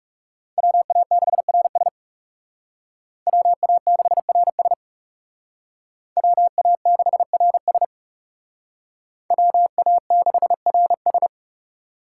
Morse code, heard as a keyed tone, about 37 wpm, 700 Hz. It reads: WA6RS WA6RS WA6RS WA6RS
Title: Speed Racing - USA 2x2 Call Signs 30wpm